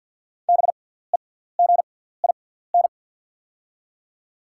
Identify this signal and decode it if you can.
Morse code, heard as a keyed tone, about 48 wpm, spaced as for 17 wpm, 700 Hz. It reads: BEGIN
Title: 6x Character Spacing - Top 500 Words - 50wpm